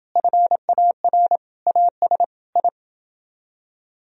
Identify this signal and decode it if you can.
Morse code, heard as a keyed tone, about 29 wpm, 700 Hz. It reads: FAR AS I